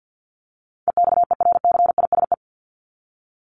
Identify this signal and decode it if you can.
Morse code, heard as a keyed tone, about 50 wpm, 700 Hz. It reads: EXERCISE
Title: Top 501-750 Words 50wpm